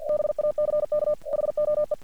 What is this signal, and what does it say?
Morse code, heard as a keyed tone, about 50 wpm, 600 Hz. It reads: BACKHOE
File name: archived / CRSnet